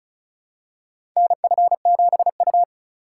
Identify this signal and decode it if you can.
Morse code, heard as a keyed tone, about 34 wpm, 690 Hz. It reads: NF7U
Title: USA Call Signs - 3x Character Spacing - 35wpm